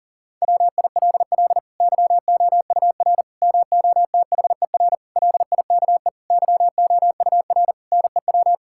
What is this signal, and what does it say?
Morse code, heard as a keyed tone, about 40 wpm, 700 Hz. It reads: WILL YOUR MOTHER LIKE YOUR NEW
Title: Sentences from Top 1000 Words - Review Part 5 - 40wpm